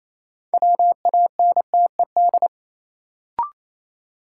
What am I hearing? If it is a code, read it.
Morse code, heard as a keyed tone, about 28 wpm, 700 Hz. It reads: WANTED
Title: Sets of 2 Words - New Word Drill - 28wpm